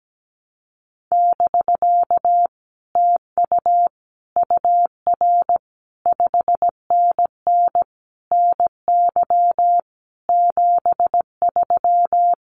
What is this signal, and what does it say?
Morse code, heard as a keyed tone, about 17 wpm, 700 Hz. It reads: <BK> TU UR 5NN NY 73
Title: Common QSO Phrases Encore 17wpm